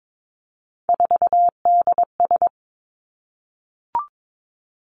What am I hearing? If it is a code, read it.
Morse code, heard as a keyed tone, about 22 wpm, 700 Hz. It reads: 4DS
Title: Three Challenging Letter-Number Combinations 22wpm